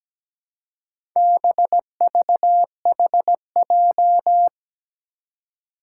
Morse code, heard as a keyed tone, about 17 wpm, 700 Hz. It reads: BVHJ
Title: Four Challenging Letter-Number Combinations 17wpm